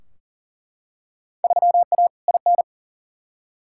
Morse code, heard as a keyed tone, about 40 wpm, 700 Hz. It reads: 3A IN
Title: Field Day Exchanges - 40wpm